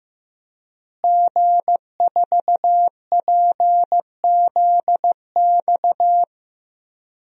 Morse code, heard as a keyed tone, about 15 wpm, 700 Hz. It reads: G4PZX
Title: DXCC-100 Call Signs - 5 Characters - Repeated 3x - 15wpm